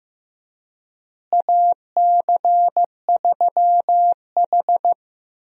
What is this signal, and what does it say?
Morse code, heard as a keyed tone, about 15 wpm, 700 Hz. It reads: AC3H
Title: USA Call Signs - 4 Characters - Repeated 2x - 15wpm